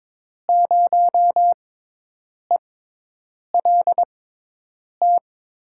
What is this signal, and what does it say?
Morse code, heard as a keyed tone, about 22 wpm, spaced as for 8 wpm, 700 Hz. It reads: 0ELT
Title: USA Call Signs - 5 Characters - 6x Character Spacing - 22wpm